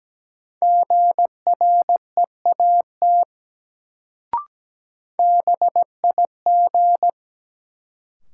Morse code, heard as a keyed tone, about 17 wpm, 700 Hz. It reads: GREAT BIG
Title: Sets of 2 Words - New Word Drill - 17wpm